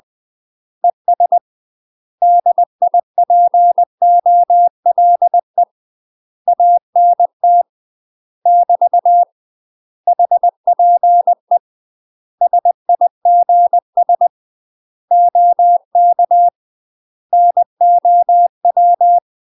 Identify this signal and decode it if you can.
Morse code, heard as a keyed tone, about 20 wpm, 700 Hz. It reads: ES DIPOLE ANT <BT> HPE SIGS OK NOW
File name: UK MASTER QSOs TRAINING / 20WPM / UK MASTER QSOs 0038